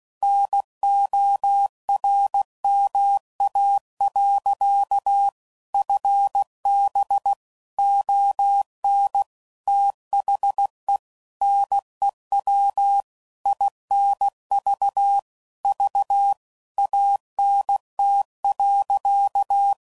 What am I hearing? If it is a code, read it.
Morse code, heard as a keyed tone, about 16 wpm, 800 Hz. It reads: NORMA. FB ON THE NEW INV V ANT.
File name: CW Code Groups @15WPM / QSOs / WD0FBT Spaced